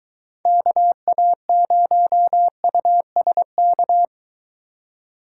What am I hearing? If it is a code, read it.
Morse code, heard as a keyed tone, about 21 wpm, 700 Hz. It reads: KA0USK